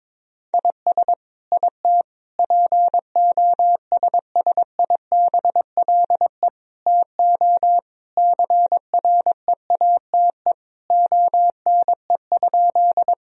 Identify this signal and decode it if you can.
Morse code, heard as a keyed tone, about 22 wpm, 700 Hz. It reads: IS IT POSSIBLE TO CREATE ONE?
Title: Sentences from Top 1000 Words - Review Part 3 - 22wpm